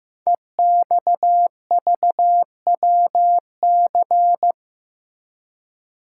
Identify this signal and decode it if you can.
Morse code, heard as a keyed tone, about 15 wpm, 700 Hz. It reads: EXVWC
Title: Code Groups - 5 Letters - 15wpm